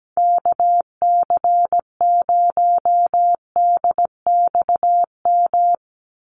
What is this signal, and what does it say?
Morse code, heard as a keyed tone, about 17 wpm, 700 Hz. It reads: KC0DXM
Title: USA Call Signs 17wpm